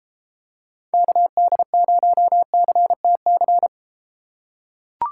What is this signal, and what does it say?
Morse code, heard as a keyed tone, about 32 wpm, 700 Hz. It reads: KD0CTC